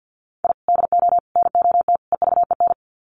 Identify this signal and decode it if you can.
Morse code, heard as a keyed tone, about 50 wpm, 700 Hz. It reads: I DO NOT EVEN
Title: Sets of 4 Words 50wpm